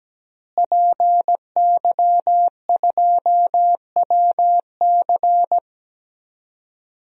Morse code, heard as a keyed tone, about 17 wpm, 700 Hz. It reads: PY2WC
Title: DXCC-100 Call Signs - 5 Characters - Repeated 2x - 17wpm